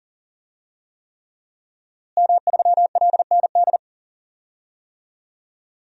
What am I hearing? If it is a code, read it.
Morse code, heard as a keyed tone, about 40 wpm, 700 Hz. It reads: M3LND
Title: DXCC-100 Call Signs - 5 Characters - Repeated 3x - 40wpm